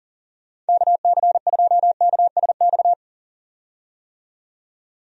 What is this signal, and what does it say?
Morse code, heard as a keyed tone, about 40 wpm, 700 Hz. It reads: KC2KSX
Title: USA Call Signs - 6 Characters - 40wpm